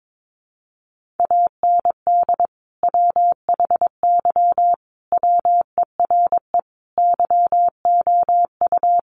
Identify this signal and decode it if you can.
Morse code, heard as a keyed tone, about 22 wpm, 700 Hz. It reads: AND WHY WERE YOU
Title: Sentences from Top 1000 Words 22wpm